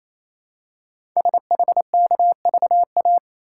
Morse code, heard as a keyed tone, about 27 wpm, 700 Hz. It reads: SHKVA
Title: Code Groups - 5 Letters - 28wpm